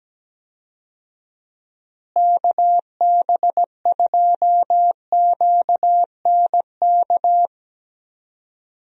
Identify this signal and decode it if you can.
Morse code, heard as a keyed tone, about 17 wpm, 700 Hz. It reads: KB2QNK
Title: USA Call Signs - 6 Characters - Repeated 3x - 17wpm